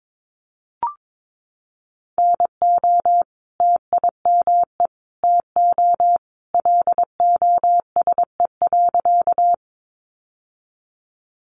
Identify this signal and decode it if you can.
Morse code, heard as a keyed tone, about 22 wpm, 700 Hz. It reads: NO TIME TO LOSE.
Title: Sentences from Top 1100 Words 22wpm